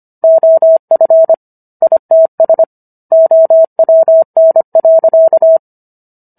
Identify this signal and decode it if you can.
Morse code, heard as a keyed tone, about 25 wpm, 650 Hz. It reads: OF ITS OWN.